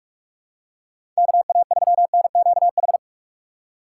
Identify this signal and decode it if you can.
Morse code, heard as a keyed tone, about 46 wpm, 700 Hz. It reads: KA3NQH
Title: USA Call Signs - 6 Characters - 6x Character Spacing - 45wpm